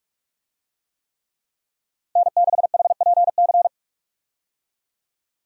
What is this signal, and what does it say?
Morse code, heard as a keyed tone, about 46 wpm, 700 Hz. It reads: N6HPC
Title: USA Call Signs - 5 Characters - 45wpm